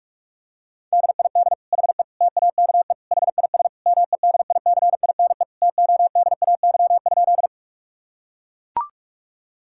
Morse code, heard as a keyed tone, about 41 wpm, 700 Hz. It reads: DID HE TAKE HIS MEDICINE TODAY?
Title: Sentences from Top 1000 Words - Review Part 6 - 45wpm